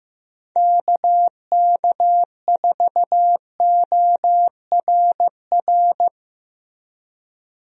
Morse code, heard as a keyed tone, about 15 wpm, 700 Hz. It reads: KK4ORR
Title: USA Call Signs - 6 Characters - 15wpm